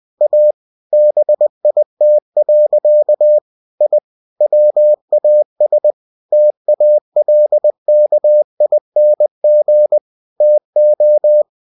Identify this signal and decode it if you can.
Morse code, heard as a keyed tone, about 20 wpm, 600 Hz. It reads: A BIT. I WAS TALKING TO
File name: cw-books / winnie-the-pooh / winnie-20wpm-page-0062